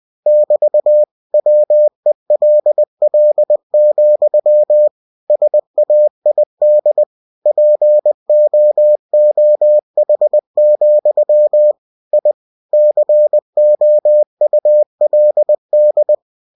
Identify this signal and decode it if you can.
Morse code, heard as a keyed tone, about 20 wpm, 600 Hz. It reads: <BT> WELL, SAID POOH, I COULD
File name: cw-books / winnie-the-pooh / winnie-20wpm-page-0015